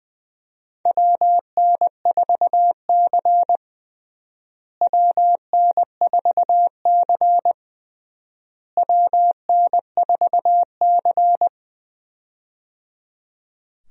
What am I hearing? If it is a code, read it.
Morse code, heard as a keyed tone, about 21 wpm, 700 Hz. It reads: WN4C WN4C WN4C
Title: USA Call Signs - 4 Characters - Repeated 3x - 20wpm